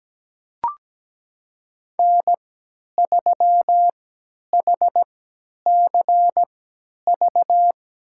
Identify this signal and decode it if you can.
Morse code, heard as a keyed tone, about 17 wpm, spaced as for 10 wpm, 700 Hz. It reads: N3HCV
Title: USA Call Signs - 5 Characters - 3x Character Spacing - 17wpm